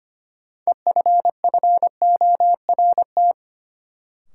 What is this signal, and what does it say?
Morse code, heard as a keyed tone, about 25 wpm, 700 Hz. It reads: EFFORT
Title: Sentences from Top 1200 Words - New Word Drill - 25wpm